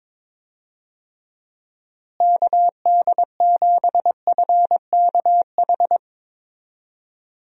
Morse code, heard as a keyed tone, about 22 wpm, 700 Hz. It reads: KD7FKH
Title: USA Call Signs - 6 Characters - Repeated 2x - 22wpm